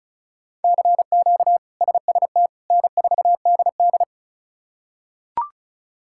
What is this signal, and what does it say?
Morse code, heard as a keyed tone, about 36 wpm, 700 Hz. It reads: CQ SST N4DD
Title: K1USN SST QSO Practice - 35wpm